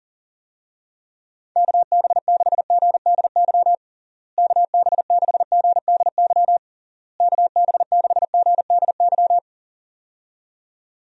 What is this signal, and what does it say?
Morse code, heard as a keyed tone, about 37 wpm, 700 Hz. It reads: KB6GDY KB6GDY KB6GDY
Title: USA Call Signs - 6 Characters - Repeated 3x - 40wpm